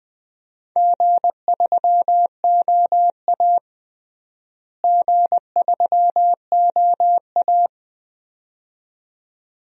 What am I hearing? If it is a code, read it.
Morse code, heard as a keyed tone, about 21 wpm, 700 Hz. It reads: G3OA G3OA
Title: DXCC-100 Call Signs - 4 Characters - Repeated 2x - 20wpm